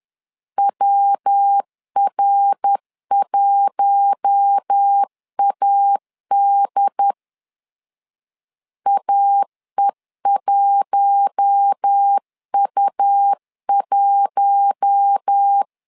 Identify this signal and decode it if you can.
Morse code, heard as a keyed tone, about 10 wpm, 800 Hz. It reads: WR1AD AE1U1
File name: LEZIONI / 4 Q1 10wpm 2448